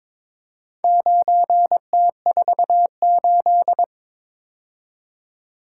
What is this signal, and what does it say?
Morse code, heard as a keyed tone, about 22 wpm, 700 Hz. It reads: 9T48
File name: practice / 22wpm / Four Letter-Number Combinations - 22wpm